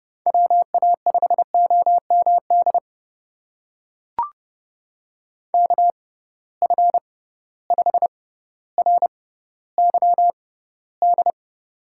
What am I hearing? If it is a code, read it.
Morse code, heard as a keyed tone, about 30 wpm, spaced as for 15 wpm, 700 Hz. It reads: WA5OMD KF5RYD
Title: USA Call Signs - 6x Character Spacing - 30wpm